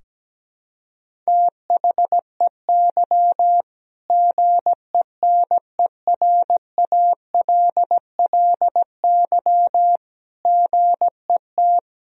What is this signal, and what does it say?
Morse code, heard as a keyed tone, about 17 wpm, 700 Hz. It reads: THEY GENERALLY GET
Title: Sentences from Top 1000 Words - Review Part 4 - 17wpm